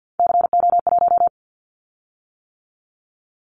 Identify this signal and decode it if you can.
Morse code, heard as a keyed tone, about 50 wpm, 700 Hz. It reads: CO1
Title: Callsign Prefixes for Top 100 DXCC - 50wpm